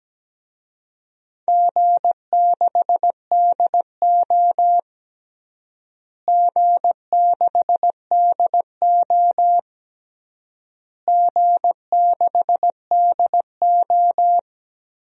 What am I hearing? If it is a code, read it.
Morse code, heard as a keyed tone, about 17 wpm, 700 Hz. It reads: G6DO G6DO G6DO
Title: DXCC-100 Call Signs - 4 Characters - Repeated 3x - 17wpm